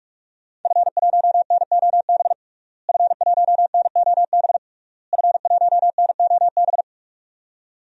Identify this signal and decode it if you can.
Morse code, heard as a keyed tone, about 44 wpm, 700 Hz. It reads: F1NOB F1NOB F1NOB
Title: DXCC-100 Call Signs - 5 Characters - Repeated 3x - 45wpm